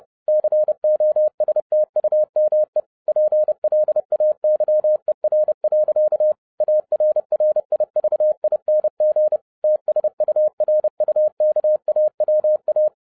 Morse code, heard as a keyed tone, about 30 wpm, 610 Hz. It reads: COSTUME PLAYER. ARRIVING TSURUKAWA